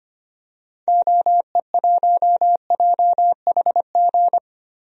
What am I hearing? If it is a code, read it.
Morse code, heard as a keyed tone, about 25 wpm, 700 Hz. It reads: OE1JHG
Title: DXCC-100 Call Signs - 6 Characters - Repeated 3x - 25wpm